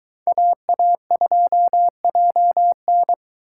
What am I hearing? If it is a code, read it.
Morse code, heard as a keyed tone, about 22 wpm, 700 Hz. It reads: AA2JN